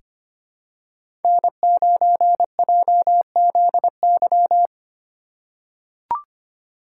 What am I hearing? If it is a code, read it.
Morse code, heard as a keyed tone, about 25 wpm, 700 Hz. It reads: N9JZY
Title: K1USN SST QSO Practice - 25wpm